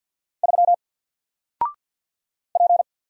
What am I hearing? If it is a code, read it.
Morse code, heard as a keyed tone, about 50 wpm, 700 Hz. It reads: 3 P
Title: Single Character (Letter:Number:Punctuation:Prosign) 50wpm